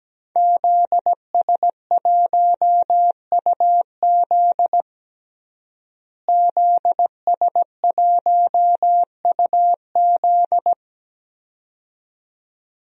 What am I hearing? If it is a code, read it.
Morse code, heard as a keyed tone, about 17 wpm, 700 Hz. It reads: ZS1UZ ZS1UZ
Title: DXCC-100 Call Signs - 5 Characters - Repeated 2x - 17wpm